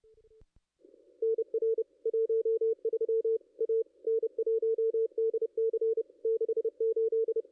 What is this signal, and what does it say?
Morse code, heard as a keyed tone, about 31 wpm, 450 Hz. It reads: NR 13 A N1DC 68
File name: contests / ss-cw-k5tr-op-n5rz -2200